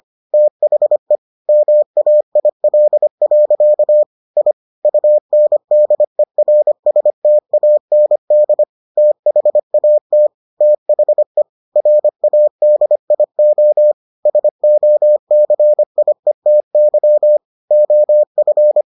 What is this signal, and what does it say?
Morse code, heard as a keyed tone, about 25 wpm, 600 Hz. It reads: THE MAIL. I UNDERSTAND THAT THE RADIO SOCIETY OF